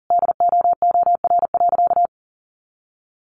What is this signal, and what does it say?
Morse code, heard as a keyed tone, about 40 wpm, 700 Hz. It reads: DOOR.
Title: A Sound Of Thunder 40wpm